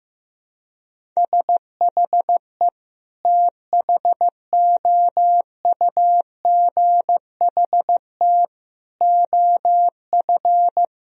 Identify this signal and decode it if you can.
Morse code, heard as a keyed tone, about 15 wpm, 700 Hz. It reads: SHE THOUGHT OF
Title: Sentences from Top 1000 Words - Review Part 1 - 15wpm